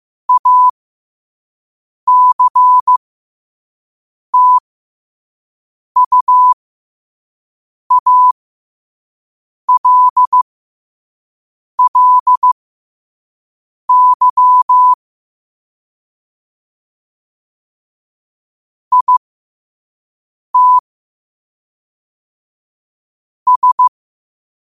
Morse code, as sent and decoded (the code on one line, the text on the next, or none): .- -.-. - ..- .- .-.. .-.. -.-- / .. - / ...
ACTUALLY IT S